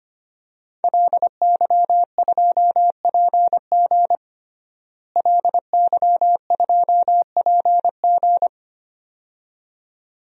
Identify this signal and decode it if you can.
Morse code, heard as a keyed tone, about 25 wpm, 700 Hz. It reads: LY2PG LY2PG
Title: DXCC-100 Call Signs - 5 Characters - Repeated 2x - 25wpm